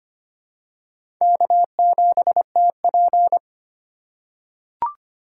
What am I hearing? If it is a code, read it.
Morse code, heard as a keyed tone, about 25 wpm, 700 Hz. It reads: K7TP
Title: USA Call Signs - 4 Characters - 3x Character Spacing - 25wpm